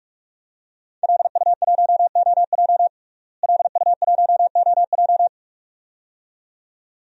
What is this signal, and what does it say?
Morse code, heard as a keyed tone, about 41 wpm, 700 Hz. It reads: LU1OJ LU1OJ
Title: DXCC-100 Call Signs - 5 Characters - Repeated 2x - 45wpm